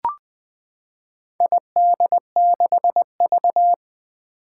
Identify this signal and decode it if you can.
Morse code, heard as a keyed tone, about 20 wpm, 700 Hz. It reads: ID6V
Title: Four Challenging Letter-Number Combinations 20wpm